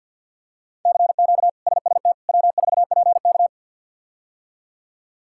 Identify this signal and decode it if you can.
Morse code, heard as a keyed tone, about 49 wpm, 700 Hz. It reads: CQ SST W4PK
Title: K1USN SST QSO Practice - 50wpm